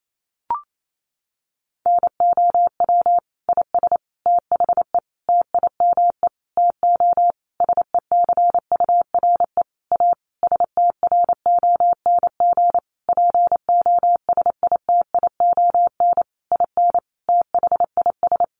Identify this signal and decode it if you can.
Morse code, heard as a keyed tone, about 28 wpm, 700 Hz. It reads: NOW IS THE TIME TO SECURE A STRONG POSITION IN THIS
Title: Sentences from Top 1200 Words 28wpm